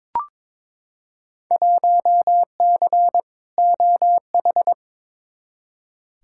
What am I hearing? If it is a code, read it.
Morse code, heard as a keyed tone, about 23 wpm, 700 Hz. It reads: 1C OH
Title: Field Day Exchanges - 22wpm